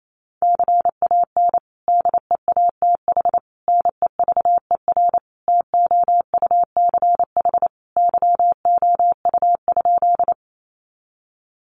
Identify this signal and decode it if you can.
Morse code, heard as a keyed tone, about 28 wpm, 700 Hz. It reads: CAN DEATH NEVER TOUCH YOU?
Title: Sentences from Top 1200 Words 28wpm